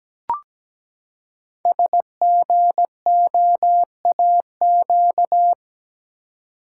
Morse code, heard as a keyed tone, about 17 wpm, 700 Hz. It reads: SGOAQ
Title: Code Groups - 5 Letters - 17wpm